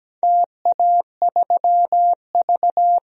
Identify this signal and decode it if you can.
Morse code, heard as a keyed tone, about 17 wpm, 700 Hz. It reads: TA3V
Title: DXCC-100 Call Signs - 4 Characters - 17wpm